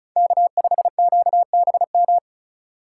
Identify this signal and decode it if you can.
Morse code, heard as a keyed tone, about 36 wpm, 700 Hz. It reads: K5QBM
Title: USA Call Signs - 5 Characters - 2x Character Spacing - 35wpm